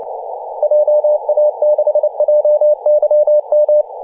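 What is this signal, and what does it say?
Morse code, heard as a keyed tone, about 28 wpm, 600 Hz. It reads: JA6JYM
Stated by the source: DX station in Papua New Guinea on 21.040 MHz in CW.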